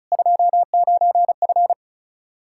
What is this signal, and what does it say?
Morse code, heard as a keyed tone, about 34 wpm, 700 Hz. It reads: 29F
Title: K1USN SST QSO Practice 35wpm